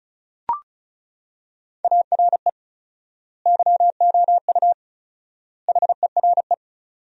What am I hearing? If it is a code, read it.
Morse code, heard as a keyed tone, about 34 wpm, 700 Hz. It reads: ARE YOU HERE
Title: 4x Word-Spacing - Sentences from Top 400 Words - 35wpm